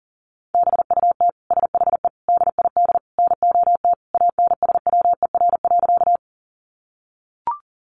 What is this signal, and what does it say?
Morse code, heard as a keyed tone, about 40 wpm, 700 Hz. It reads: BUT SHE DID NOT ANSWER.
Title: Sentences from Top 600 Words 40wpm